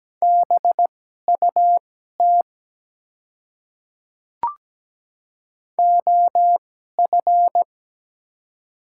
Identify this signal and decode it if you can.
Morse code, heard as a keyed tone, about 17 wpm, spaced as for 12 wpm, 700 Hz. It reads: BUT OF
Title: 2x Character Spacing - Top 100 Words - 17wpm